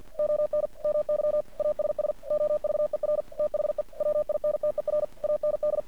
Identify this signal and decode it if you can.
Morse code, heard as a keyed tone, about 48 wpm, 600 Hz. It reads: ON MY ASS OVER THE WINTER AND
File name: archived / CRSnet